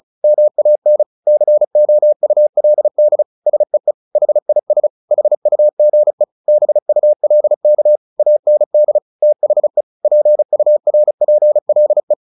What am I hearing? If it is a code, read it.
Morse code, heard as a keyed tone, about 35 wpm, 600 Hz. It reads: MAN COULD SEE HIS HUGE BULK AND THE PURPLE
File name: the old man and the sea 2 35wpm